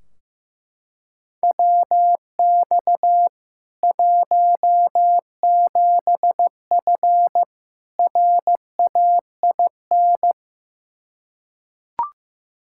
Morse code, heard as a keyed tone, about 15 wpm, 700 Hz. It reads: WX 17F RAIN
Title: K1USN SST QSO Practice 15wpm